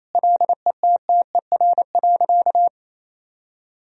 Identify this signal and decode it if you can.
Morse code, heard as a keyed tone, about 28 wpm, 700 Hz. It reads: LETTER.
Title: Sentences from Top 1000 Words - Review Part 4 - 28wpm